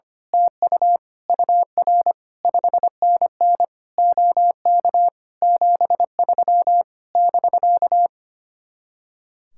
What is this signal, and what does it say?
Morse code, heard as a keyed tone, about 24 wpm, 700 Hz. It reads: TU UR 5NN OK 73 <BK>
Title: Speed-Racing Common QSO Phrases - 25wpm